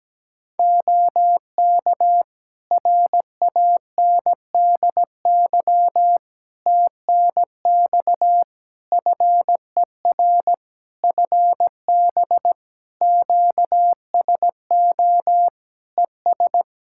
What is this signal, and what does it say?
Morse code, heard as a keyed tone, about 17 wpm, 700 Hz. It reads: OK RANDY TNX FER FB QSO ES
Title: Practice QSO - Exchange 4 - Ending - 17wpm